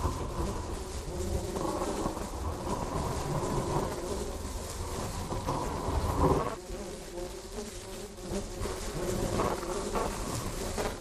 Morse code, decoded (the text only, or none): none